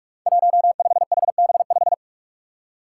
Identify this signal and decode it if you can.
Morse code, heard as a keyed tone, about 46 wpm, 700 Hz. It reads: 15HB5
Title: Five Challenging Letter-Number Combinations 45wpm